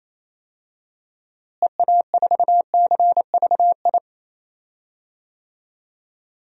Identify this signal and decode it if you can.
Morse code, heard as a keyed tone, about 28 wpm, 700 Hz. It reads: EA4CVI
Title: DXCC-100 Call Signs - 6 Characters - Repeated 2x - 28wpm